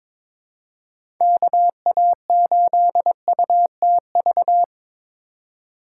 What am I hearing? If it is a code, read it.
Morse code, heard as a keyed tone, about 22 wpm, 700 Hz. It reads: KA8UTV
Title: USA Call Signs - 6 Characters - 2x Character Spacing - 22wpm